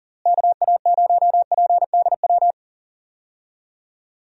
Morse code, heard as a keyed tone, about 40 wpm, 700 Hz. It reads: KA0PDW
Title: USA Call Signs - 6 Characters - Repeated 3x - 40wpm